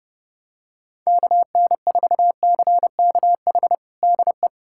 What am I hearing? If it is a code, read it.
Morse code, heard as a keyed tone, about 32 wpm, 700 Hz. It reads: KN4CKH DE
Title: USA Callsign DE Callsign 30wpm